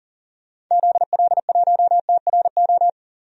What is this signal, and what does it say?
Morse code, heard as a keyed tone, about 40 wpm, 700 Hz. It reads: ZL1TRO
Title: DXCC-100 Call Signs - 6 Characters - Repeated 2x - 40wpm